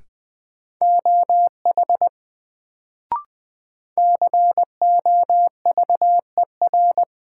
Morse code, heard as a keyed tone, about 20 wpm, 700 Hz. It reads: OH COVER
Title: Sentences from Top 500 Words - New Word Drill - 20wpm